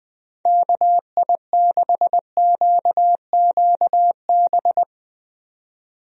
Morse code, heard as a keyed tone, about 20 wpm, 700 Hz. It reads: KI6QQB
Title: USA Call Signs - 6 Characters - 20wpm